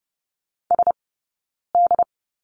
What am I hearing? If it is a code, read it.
Morse code, heard as a keyed tone, about 30 wpm, spaced as for 9 wpm, 690 Hz. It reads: SD
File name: Lesson 17d - Letter u - Two Characters 30wpm